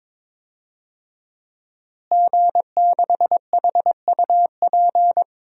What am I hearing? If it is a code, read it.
Morse code, heard as a keyed tone, about 21 wpm, 700 Hz. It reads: G6HUP
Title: DXCC-100 Call Signs - 5 Characters - Repeated 2x - 22wpm